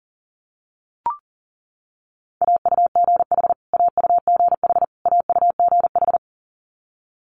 Morse code, heard as a keyed tone, about 38 wpm, 700 Hz. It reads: AUGH AUGH AUGH
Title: Multi-letter Phonemes Repeated 3x 40wpm